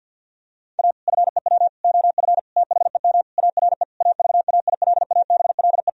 Morse code, heard as a keyed tone, about 50 wpm, 700 Hz. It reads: A FEW OF THEM ARE AVAILABLE
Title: Sentences from Top 1000 Words - Review Part 3 - 50wpm